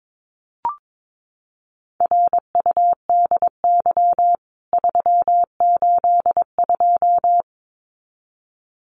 Morse code, heard as a keyed tone, about 22 wpm, 700 Hz. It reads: RUDY 382
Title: CWT Exchange 22wpm